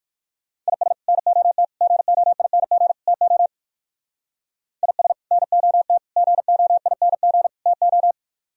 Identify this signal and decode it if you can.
Morse code, heard as a keyed tone, about 58 wpm, 700 Hz. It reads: IS NOT GOING TO IS NOT GOING TO